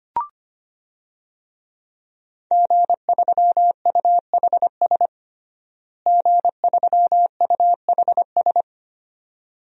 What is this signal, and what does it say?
Morse code, heard as a keyed tone, about 25 wpm, 700 Hz. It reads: G3UHS G3UHS
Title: DXCC-100 Call Signs - 5 Characters - Repeated 2x - 25wpm